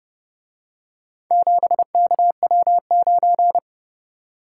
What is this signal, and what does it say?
Morse code, heard as a keyed tone, about 28 wpm, 700 Hz. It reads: 7KW9